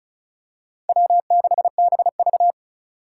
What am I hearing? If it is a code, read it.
Morse code, heard as a keyed tone, about 36 wpm, 700 Hz. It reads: W6BV
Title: Four Challenging Letter-Number Combinations 35wpm